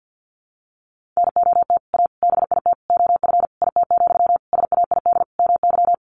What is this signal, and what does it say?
Morse code, heard as a keyed tone, about 50 wpm, 700 Hz. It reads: NOT A BIT OF IT, SAID MY
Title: Sentences from Top 1400 Words 50wpm